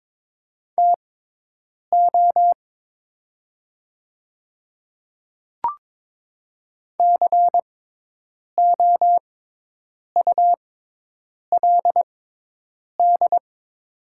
Morse code, heard as a keyed tone, about 22 wpm, spaced as for 8 wpm, 700 Hz. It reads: TO COULD
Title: 6x Character Spacing - Top 100 Words - 22wpm